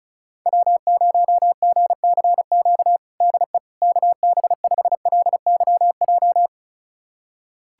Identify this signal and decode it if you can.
Morse code, heard as a keyed tone, about 36 wpm, 700 Hz. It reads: W0GCQ DE KB5LYJ
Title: USA Callsign DE Callsign 35wpm